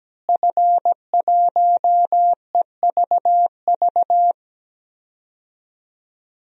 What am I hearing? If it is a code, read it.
Morse code, heard as a keyed tone, about 17 wpm, 700 Hz. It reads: F1EVV
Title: DXCC-100 Call Signs - 5 Characters - Repeated 2x - 17wpm